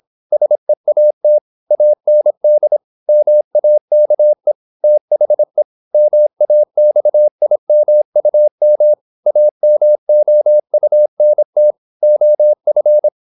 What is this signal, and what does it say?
Morse code, heard as a keyed tone, about 26 wpm, 600 Hz. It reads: SEAT AND MAKE THE MAXIMUM AMOUNT OF
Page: Never 26wpm | CW med Gnister